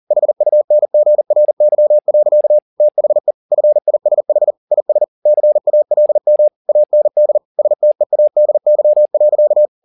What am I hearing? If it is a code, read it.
Morse code, heard as a keyed tone, about 40 wpm, 600 Hz. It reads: HUNGRY. THE FISH IS CALM AND STEADY.